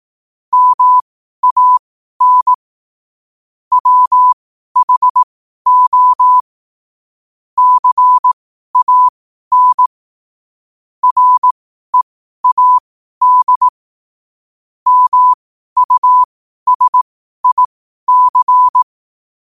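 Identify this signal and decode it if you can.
Morse code, heard as a keyed tone, about 18 wpm, spaced as for 12 wpm, 1000 Hz. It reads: MAN WHO CAN READ MUSIC
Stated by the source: 12 wpm Morse.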